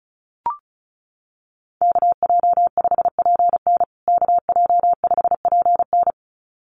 Morse code, heard as a keyed tone, about 34 wpm, 700 Hz. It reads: KJ5PN KJ5PN
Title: USA Call Signs Repeated 2x 35wpm